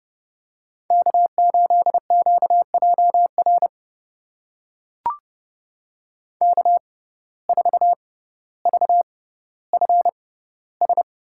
Morse code, heard as a keyed tone, about 30 wpm, spaced as for 15 wpm, 700 Hz. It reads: K8QJR K4VFS
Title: USA Call Signs - 5 Characters - 6x Character Spacing - 30wpm